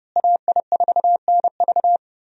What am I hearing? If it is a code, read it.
Morse code, heard as a keyed tone, about 30 wpm, 700 Hz. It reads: AI4NV
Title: USA Call Signs - 5 Characters - Repeated 2x - 30wpm